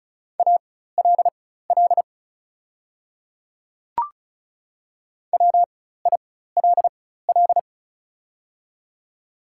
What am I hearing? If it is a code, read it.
Morse code, heard as a keyed tone, about 36 wpm, spaced as for 16 wpm, 700 Hz. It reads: ALL WILL
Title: 4x Character Spacing - Top 100 Words - 35wpm